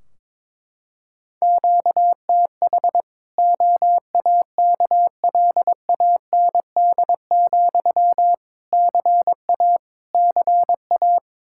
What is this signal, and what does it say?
Morse code, heard as a keyed tone, about 22 wpm, 700 Hz. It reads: QTH OAKLAND, CA CA
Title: Practice QSO - Exchange 1 - Beginning - 22wpm